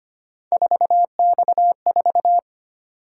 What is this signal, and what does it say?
Morse code, heard as a keyed tone, about 25 wpm, 700 Hz. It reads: 4X4
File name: Three Letter-Number Combinations - 25wpm